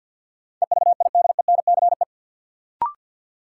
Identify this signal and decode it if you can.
Morse code, heard as a keyed tone, about 52 wpm, 700 Hz. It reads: EVIDENCE
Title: Sentences from Top 1300 Words - New Word Drill Alt - 50wpm